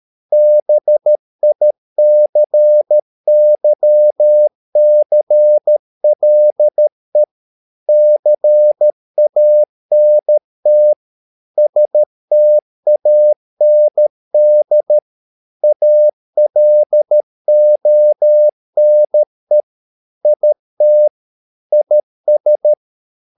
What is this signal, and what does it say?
Morse code, heard as a keyed tone, about 13 wpm, 600 Hz. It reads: BICYCLE CANT STAND ALONE IT IS